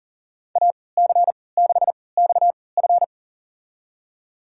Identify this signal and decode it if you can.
Morse code, heard as a keyed tone, about 40 wpm, spaced as for 23 wpm, 700 Hz. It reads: AC6XF
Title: USA Call Signs - 5 Characters - 3x Character Spacing - 40wpm